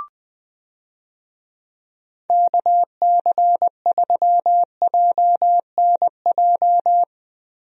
Morse code, heard as a keyed tone, about 21 wpm, 700 Hz. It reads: KC3JNJ
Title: USA Call Signs - 6 Characters - Repeated 2x - 20wpm